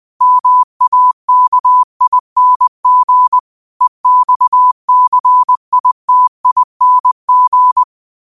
Morse code, heard as a keyed tone, about 20 wpm, 1000 Hz. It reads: MAKING EXCITING